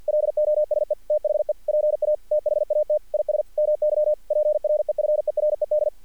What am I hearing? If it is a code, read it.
Morse code, heard as a keyed tone, about 46 wpm, 600 Hz. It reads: LOSE THE PA THAT IS MY PREFERED
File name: archived / CRSnet